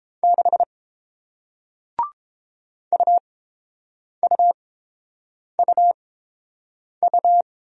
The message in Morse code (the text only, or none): 6 UUUU